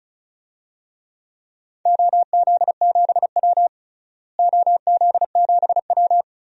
Morse code, heard as a keyed tone, about 36 wpm, 700 Hz. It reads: OZ7W OZ7W
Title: DXCC-100 Call Signs - 4 Characters - Repeated 2x - 35wpm